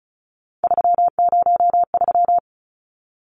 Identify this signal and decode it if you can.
Morse code, heard as a keyed tone, about 34 wpm, 700 Hz. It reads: 303
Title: Numbers - 3-Digit 35wpm